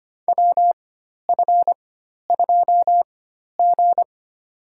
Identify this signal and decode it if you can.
Morse code, heard as a keyed tone, about 25 wpm, spaced as for 12 wpm, 700 Hz. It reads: WF2G
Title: USA Call Signs - 4 Characters - 4x Character Spacing - 25wpm